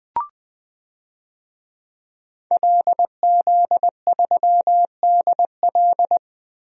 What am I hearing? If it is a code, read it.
Morse code, heard as a keyed tone, about 21 wpm, 700 Hz. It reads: LZ3DL
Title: DXCC-100 Call Signs - 5 Characters - 20wpm